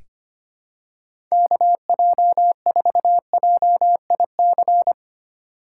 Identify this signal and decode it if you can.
Morse code, heard as a keyed tone, about 25 wpm, 700 Hz. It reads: KJ4JIC
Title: USA Call Signs - 6 Characters - 6x Character Spacing - 25wpm